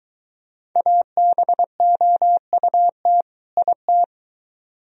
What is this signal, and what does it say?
Morse code, heard as a keyed tone, about 23 wpm, 700 Hz. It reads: ABOUT IT